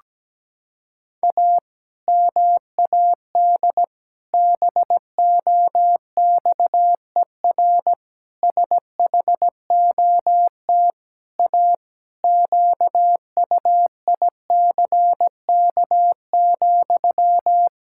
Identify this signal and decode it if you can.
Morse code, heard as a keyed tone, about 17 wpm, 700 Hz. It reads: A MAD BOXER SHOT A QUICK,
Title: Pangram Challenge v2 - 17wpm